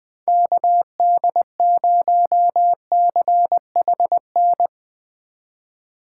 Morse code, heard as a keyed tone, about 21 wpm, 700 Hz. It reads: KD0CHN
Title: Speed Racing - USA 2x3 Call Signs 20wpm